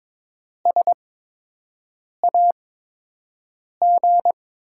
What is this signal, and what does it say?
Morse code, heard as a keyed tone, about 22 wpm, spaced as for 6 wpm, 700 Hz. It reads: SAG
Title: USA Call Signs - 5 Characters - 8x Character Spacing - 22wpm